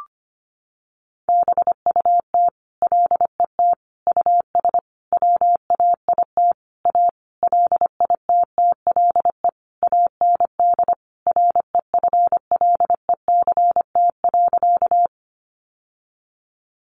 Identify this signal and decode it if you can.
Morse code, heard as a keyed tone, about 25 wpm, 700 Hz. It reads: BUT LET US WAIT A LITTLE AND REFLECT.
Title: Sentences from Top 1500 Words 25wpm